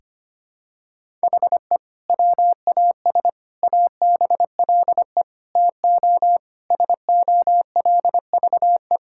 Morse code, heard as a keyed tone, about 25 wpm, 700 Hz. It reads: HE WAS ABLE TO SOLVE
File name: practice / 25wpm / Sentences with 3-Word Phrases - 25wpm